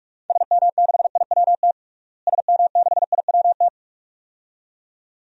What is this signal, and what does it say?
Morse code, heard as a keyed tone, about 47 wpm, 700 Hz. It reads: SM6IWT SM6IWT
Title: DXCC-100 Call Signs - 6 Characters - Repeated 2x - 45wpm